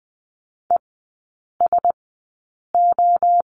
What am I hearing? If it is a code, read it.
Morse code, heard as a keyed tone, about 20 wpm, spaced as for 8 wpm, 700 Hz. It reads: ESO
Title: 041 Lesson 8e - Number 1 - Three Characters 20wpm